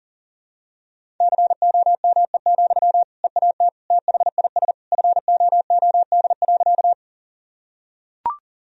Morse code, heard as a keyed tone, about 40 wpm, 700 Hz. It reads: COME, EAT THIS FOOD.
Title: Sentences from Top 1000 Words - Review Part 1 - 40wpm